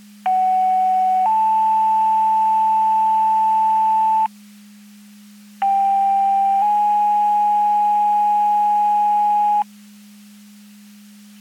Two-tone page, heard as a one-second tone, then a three-second tone, about 750 then 880 Hz, about 790 then 830 Hz. It is heard from 0.3 to 9.6 s.
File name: Portage 154.385 Paging